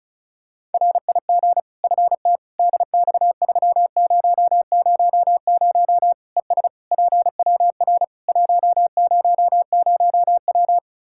Morse code, heard as a keyed tone, about 34 wpm, 700 Hz. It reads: RIG FT DX3000 ES PWR 100W
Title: K1USN SST QSO Practice 35wpm